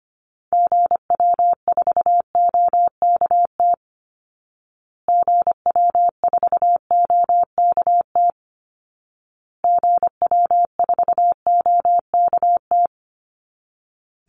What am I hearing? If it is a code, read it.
Morse code, heard as a keyed tone, about 25 wpm, 700 Hz. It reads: GW4OKT GW4OKT GW4OKT
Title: Call Signs Repeated 2x - 25wpm